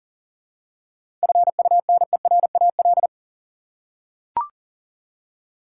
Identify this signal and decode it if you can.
Morse code, heard as a keyed tone, about 40 wpm, 700 Hz. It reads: FUNERAL
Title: Seven Letter Words - 40wpm